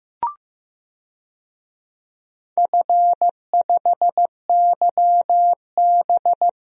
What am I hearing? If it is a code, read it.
Morse code, heard as a keyed tone, about 15 wpm, 700 Hz. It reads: F5YB
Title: DXCC-100 Call Signs - 4 Characters - Repeated 3x - 15wpm